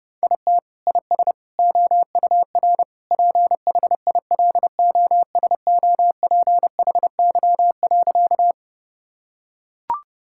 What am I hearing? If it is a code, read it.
Morse code, heard as a keyed tone, about 30 wpm, 700 Hz. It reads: IT IS OUR PHILOSOPHY.
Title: Sentences from Top 1000 Words - Review Part 6 - 30wpm